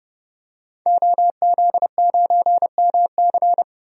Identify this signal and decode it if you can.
Morse code, heard as a keyed tone, about 30 wpm, 700 Hz. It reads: OZ9MC
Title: DXCC-100 Call Signs - 5 Characters - 30wpm